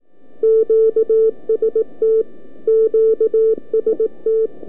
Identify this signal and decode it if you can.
Morse code, heard as a keyed tone, about 18 wpm, 440 Hz. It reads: QST QST
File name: k5cm-40mCallup-by-abi